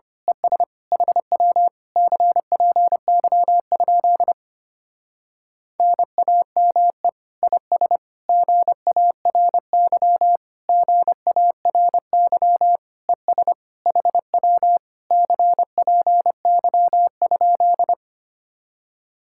Morse code, heard as a keyed tone, about 25 wpm, 700 Hz. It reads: ES HW CPY? NAME IS GARY GARY ES HW CPY?
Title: Speed-Racing Common QSO Phrases - 25wpm